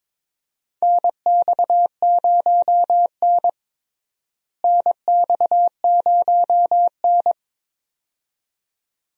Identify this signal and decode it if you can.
Morse code, heard as a keyed tone, about 22 wpm, 700 Hz. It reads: NX0N NX0N
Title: USA Call Signs - 4 Characters - Repeated 2x - 22wpm